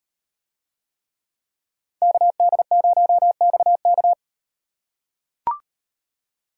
Morse code, heard as a keyed tone, about 38 wpm, 700 Hz. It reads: KD0XK